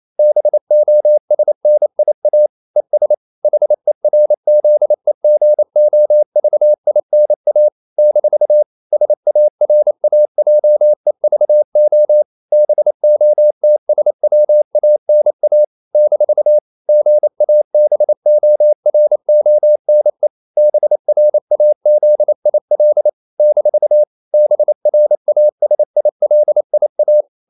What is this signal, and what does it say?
Morse code, heard as a keyed tone, about 28 wpm, 600 Hz. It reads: BOSNIA ES HERZEGOVINA - SARAJEVO BOTSWANA - GABORONE BRAZIL - BRASILIA
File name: CW / Practice / NationalCapitals28